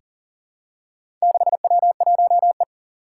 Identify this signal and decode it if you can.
Morse code, heard as a keyed tone, about 38 wpm, 700 Hz. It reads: 6W1E
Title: Four Challenging Letter-Number Combinations 40wpm